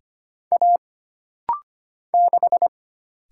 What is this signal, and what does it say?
Morse code, heard as a keyed tone, about 25 wpm, 700 Hz. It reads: A 6
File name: Single Character (Letter:Number:Punctuation:Prosign) - Rapid-Fire - 25wpm